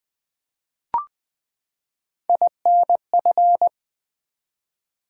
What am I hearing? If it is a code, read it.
Morse code, heard as a keyed tone, about 20 wpm, 700 Hz. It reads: INF
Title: Build A Word - 3 Letters At A Time - 9-Letter Words - 20wpm